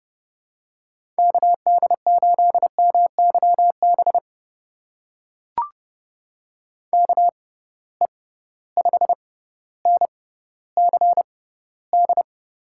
Morse code, heard as a keyed tone, about 30 wpm, spaced as for 15 wpm, 700 Hz. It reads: KD8MYB KE5NCD
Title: USA Call Signs - 6 Characters - 6x Character Spacing - 30wpm